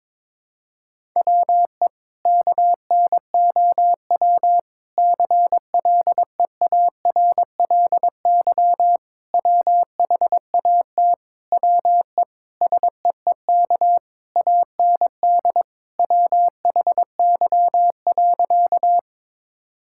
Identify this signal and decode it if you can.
Morse code, heard as a keyed tone, about 22 wpm, 700 Hz. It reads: WE KNOW CLEARLY WHAT WE SEEK AND WHY.
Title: Sentences from Top 1600 Words 22wpm